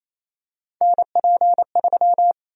Morse code, heard as a keyed tone, about 28 wpm, 700 Hz. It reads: NP3
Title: USA Call Sign Prefixes - Repeated 2x - 28wpm